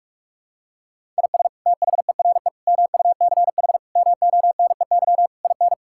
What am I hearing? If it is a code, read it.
Morse code, heard as a keyed tone, about 41 wpm, 700 Hz. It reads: IS THERE MUCH MONEY IN
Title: Sentences from Top 1000 Words - Review Part 5 - 45wpm